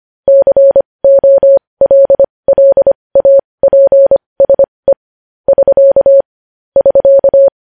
Morse code, heard as a keyed tone, about 25 wpm, spaced as for 20 wpm, 550 Hz. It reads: COLLAPSE <SK> <SK>